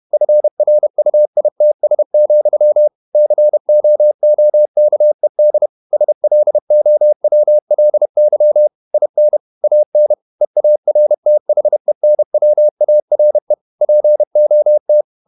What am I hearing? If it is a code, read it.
Morse code, heard as a keyed tone, about 31 wpm, 600 Hz. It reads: FRUITS, COOKED SLOWLY IN AN EARTHENWARE POT
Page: Never 31wpm | CW med Gnister